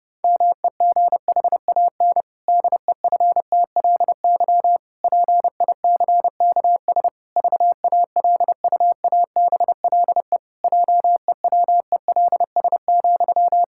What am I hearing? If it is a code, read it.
Morse code, heard as a keyed tone, about 30 wpm, 700 Hz. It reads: MEGHAN DEFTLY PICKS VALUABLE JEWELS,
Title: Pangram Challenge v2 - 30wpm